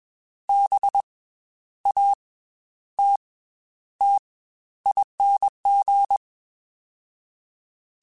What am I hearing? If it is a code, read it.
Morse code, heard as a keyed tone, about 21 wpm, spaced as for 11 wpm, 800 Hz. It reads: BATTING
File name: Special INGs / Special INGs-mp3s / Slow ing @20